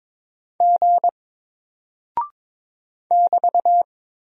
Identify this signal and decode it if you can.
Morse code, heard as a keyed tone, about 22 wpm, 700 Hz. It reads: G <BT>
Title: Single Character (Letter:Number:Punctuation:Prosign) Alt 22wpm